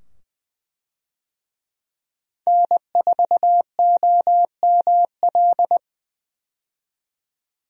Morse code, heard as a keyed tone, about 20 wpm, 700 Hz. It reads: N4OML
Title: USA Call Signs - 5 Characters - 20wpm